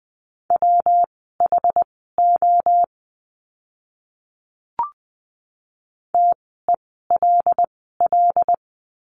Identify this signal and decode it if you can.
Morse code, heard as a keyed tone, about 20 wpm, spaced as for 14 wpm, 700 Hz. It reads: WHO TELL
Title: 230 Lesson 37f - Letter x - Words 20wpm